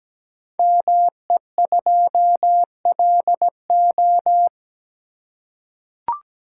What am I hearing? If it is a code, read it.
Morse code, heard as a keyed tone, about 17 wpm, 700 Hz. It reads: ME2LO
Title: Code Groups - 5 Letter:Numbers - 17wpm